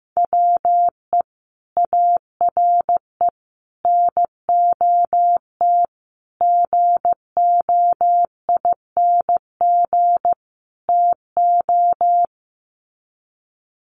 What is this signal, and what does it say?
Morse code, heard as a keyed tone, about 15 wpm, 700 Hz. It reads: WE ARE NOT GOING TO
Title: Sets of 5 Words 15wpm